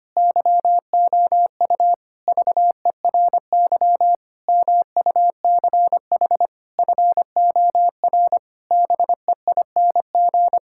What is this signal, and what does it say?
Morse code, heard as a keyed tone, about 25 wpm, 700 Hz. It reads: YOU VERY MUCH FOR BEING
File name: practice / 25wpm / Sentences with 9-Word Phrases - 25wpm